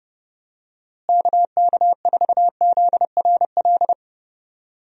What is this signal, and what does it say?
Morse code, heard as a keyed tone, about 30 wpm, 700 Hz. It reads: KK4ZRL
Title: USA Call Signs - 6 Characters - Repeated 2x - 30wpm